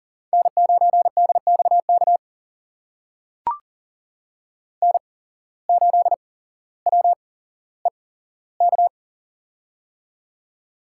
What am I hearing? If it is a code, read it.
Morse code, heard as a keyed tone, about 40 wpm, spaced as for 17 wpm, 700 Hz. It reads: N9DXK N8WEK
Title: USA Call Signs - 5 Characters - 8x Character Spacing - 40wpm